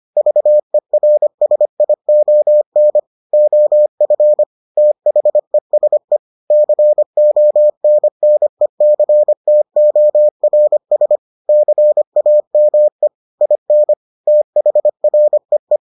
Morse code, heard as a keyed tone, about 25 wpm, 600 Hz. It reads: VERSION OF THESE CONNECTORS CAME IN THREE